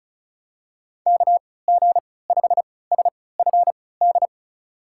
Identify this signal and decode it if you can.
Morse code, heard as a keyed tone, about 36 wpm, spaced as for 20 wpm, 700 Hz. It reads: KG5SFD
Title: USA Call Signs - 6 Characters - 3x Character Spacing - 35wpm